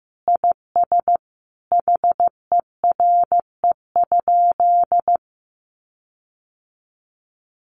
Morse code, heard as a keyed tone, about 14 wpm, 700 Hz. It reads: IS HERE?
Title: Sentences from Top 1400 Words 15wpm